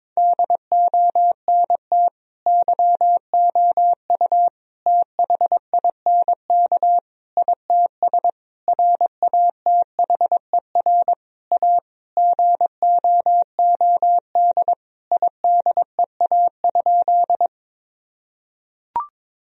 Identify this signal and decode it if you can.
Morse code, heard as a keyed tone, about 21 wpm, 700 Hz. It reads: DONT YOU THINK ITS RATHER A GOOD IDEA?
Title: Sentences from Top 1000 Words - Review Part 4 - 22wpm